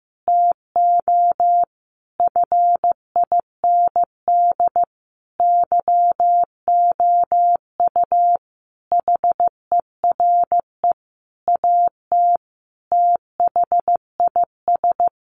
Morse code, heard as a keyed tone, about 15 wpm, 700 Hz. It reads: TO FIND YOU HERE AT THIS
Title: Sentences from Top 1400 Words 15wpm